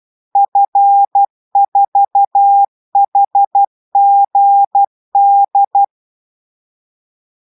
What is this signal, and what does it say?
Morse code, heard as a keyed tone, about 12 wpm, 800 Hz. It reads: F4HGD